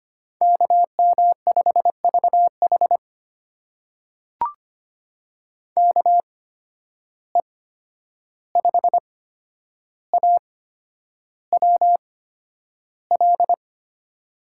Morse code, heard as a keyed tone, about 25 wpm, spaced as for 10 wpm, 700 Hz. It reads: KM5VH KE5AWL
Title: USA Call Signs - 8x Character Spacing - 25wpm